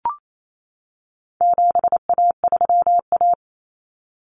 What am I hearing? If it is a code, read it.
Morse code, heard as a keyed tone, about 28 wpm, 700 Hz. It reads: 7A3A
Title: Common Cut Numbers - 4-Digit 28wpm